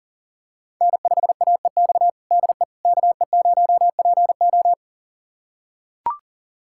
Morse code, heard as a keyed tone, about 37 wpm, 700 Hz. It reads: N5AEX DE KE0PO
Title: USA Callsign DE Callsign - 5 Characters - 40wpm